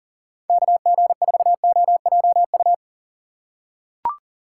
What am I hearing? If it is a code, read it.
Morse code, heard as a keyed tone, about 40 wpm, 700 Hz. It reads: KG4OJU
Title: USA Call Signs - 4x Character Spacing - 40wpm